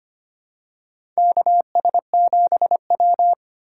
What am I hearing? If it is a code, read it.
Morse code, heard as a keyed tone, about 25 wpm, 700 Hz. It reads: KS7W
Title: USA Call Signs - 4 Characters - 4x Character Spacing - 25wpm